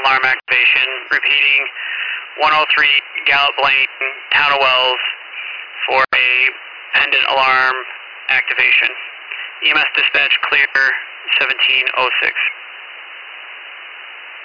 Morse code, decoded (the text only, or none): none